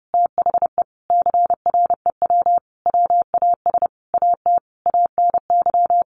Morse code, heard as a keyed tone, about 30 wpm, 700 Hz. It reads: THE CREW WAS AT ANY
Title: Sentences from Top 1500 Words 30wpm